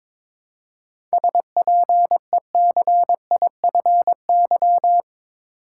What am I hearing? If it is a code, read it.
Morse code, heard as a keyed tone, about 22 wpm, 700 Hz. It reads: SPECIFY
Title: Seven Letter Words - 22wpm